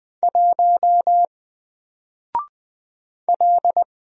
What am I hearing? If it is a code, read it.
Morse code, heard as a keyed tone, about 20 wpm, 700 Hz. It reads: 1 L
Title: Single Letter-Number Alt 20wpm